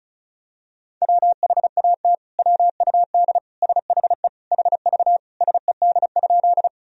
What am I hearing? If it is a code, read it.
Morse code, heard as a keyed tone, about 34 wpm, 700 Hz. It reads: WHAT WUD SHE HV SED?
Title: Sentences from Top 100 Words and Abbreviations 35wpm